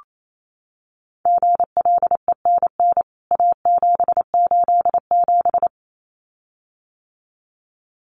Morse code, heard as a keyed tone, about 28 wpm, 700 Hz. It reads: GLENN A787
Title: CWT Exchange 28wpm